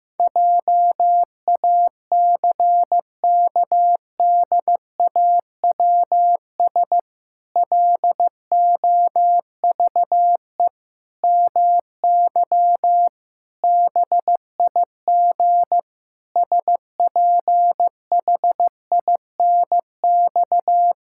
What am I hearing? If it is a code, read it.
Morse code, heard as a keyed tone, about 15 wpm, 700 Hz. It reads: JACKDAWS LOVE MY BIG SPHINX
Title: Pangram Challenge v2 - 15wpm